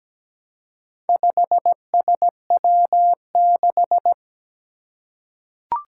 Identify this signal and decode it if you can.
Morse code, heard as a keyed tone, about 17 wpm, 700 Hz. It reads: 5SW6
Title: Four Challenging Letter-Number Combinations 17wpm